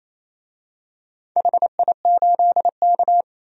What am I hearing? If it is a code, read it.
Morse code, heard as a keyed tone, about 28 wpm, 700 Hz. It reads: HI8K
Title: DXCC-100 Call Signs - 4 Characters - 28wpm